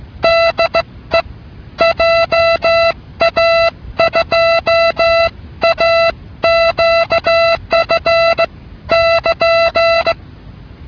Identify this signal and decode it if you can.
Morse code, heard as a keyed tone, about 16 wpm, spaced as for 14 wpm, 670 Hz. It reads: DE JA2AQF <KN>